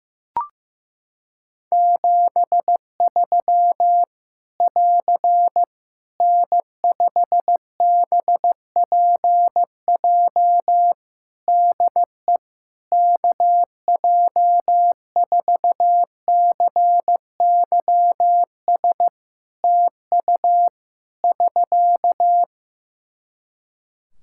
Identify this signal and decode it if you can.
Morse code, heard as a keyed tone, about 15 wpm, 700 Hz. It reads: 73 <AR> N5BPJ DE KJ4CYS TU <SK>
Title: Practice QSO - Exchange 4 - Ending - 15wpm